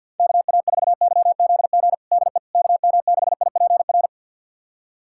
Morse code, heard as a keyed tone, about 48 wpm, 700 Hz. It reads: KA4YZG DE KM6IPR
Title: USA Callsign DE Callsign - 6 Characters - 50wpm